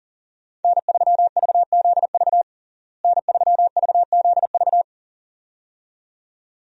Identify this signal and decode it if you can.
Morse code, heard as a keyed tone, about 37 wpm, 700 Hz. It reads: N3VZV N3VZV
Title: USA Call Signs - 5 Characters - Repeated 2x - 40wpm